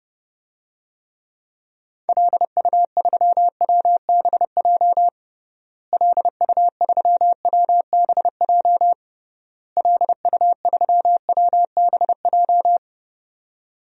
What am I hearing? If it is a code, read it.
Morse code, heard as a keyed tone, about 30 wpm, 700 Hz. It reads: LU3WBJ LU3WBJ LU3WBJ
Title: DXCC-100 Call Signs - 6 Characters - Repeated 3x - 30wpm